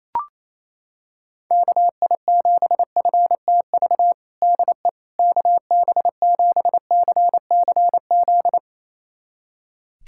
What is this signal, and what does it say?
Morse code, heard as a keyed tone, about 28 wpm, 700 Hz. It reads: KI7FTV DE KB7CCZ
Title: USA Callsign DE Callsign - 6 Characters - 28wpm